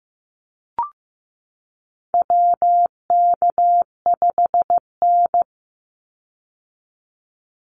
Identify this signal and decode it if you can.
Morse code, heard as a keyed tone, about 15 wpm, 700 Hz. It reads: WK5N
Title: Call Signs Repeated 3x - 15wpm